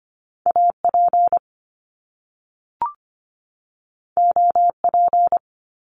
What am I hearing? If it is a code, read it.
Morse code, heard as a keyed tone, about 25 wpm, 700 Hz. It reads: AP OP
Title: Top Two Letter Combinations in Words 25wpm